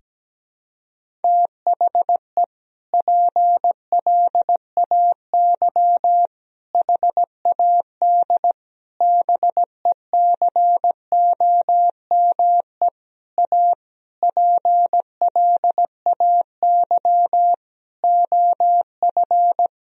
Morse code, heard as a keyed tone, about 17 wpm, 700 Hz. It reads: THE PLAY HAD BECOME A PLAY OF
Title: Sentences from Top 1000 Words - Review Part 4 - 17wpm